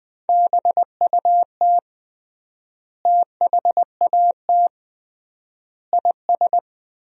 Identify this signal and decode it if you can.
Morse code, heard as a keyed tone, about 20 wpm, 700 Hz. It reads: BUT THAT IS
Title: 3x Word-Spacing - Sentences from Top 500 Words - 20wpm